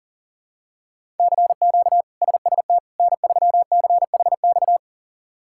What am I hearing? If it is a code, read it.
Morse code, heard as a keyed tone, about 40 wpm, 700 Hz. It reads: CQ SST N3CHX
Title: K1USN SST QSO Practice - 40wpm